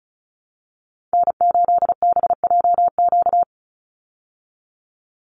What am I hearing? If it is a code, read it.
Morse code, heard as a keyed tone, about 34 wpm, 700 Hz. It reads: N8BJQ
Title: Call Signs Repeated 3x - 35wpm